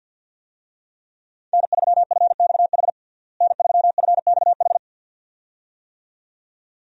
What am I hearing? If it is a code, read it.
Morse code, heard as a keyed tone, about 51 wpm, 700 Hz. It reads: N3FXH N3FXH
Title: USA Call Signs - 5 Characters - Repeated 2x - 50wpm